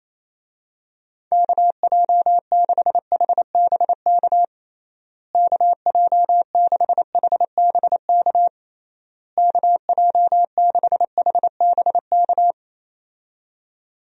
Morse code, heard as a keyed tone, about 28 wpm, 700 Hz. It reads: KJ6HBK KJ6HBK KJ6HBK
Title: USA Call Signs - 6 Characters - Repeated 3x - 28wpm